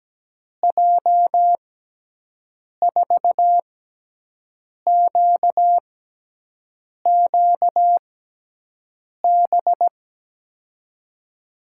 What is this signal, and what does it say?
Morse code, heard as a keyed tone, about 17 wpm, spaced as for 6 wpm, 700 Hz. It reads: J4QQB
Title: USA Call Signs - 6x Character Spacing - 17wpm